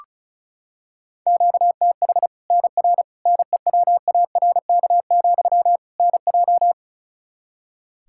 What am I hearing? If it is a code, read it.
Morse code, heard as a keyed tone, about 36 wpm, 700 Hz. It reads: QTH NR NEWARK, NJ
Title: Practice QSO - Exchange 1 - Beginning - 35wpm